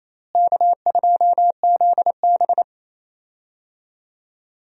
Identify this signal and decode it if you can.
Morse code, heard as a keyed tone, about 27 wpm, 700 Hz. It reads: K2ZB
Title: USA Call Signs - 4 Characters - Repeated 3x - 28wpm